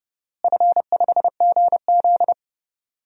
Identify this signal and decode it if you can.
Morse code, heard as a keyed tone, about 30 wpm, 700 Hz. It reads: F5GZ
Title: DXCC-100 Call Signs - 4 Characters - Repeated 3x - 30wpm